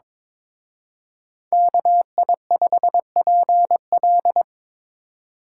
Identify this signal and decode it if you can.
Morse code, heard as a keyed tone, about 25 wpm, 700 Hz. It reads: KI5PL
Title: Speed Racing - USA 2x2 Call Signs 22wpm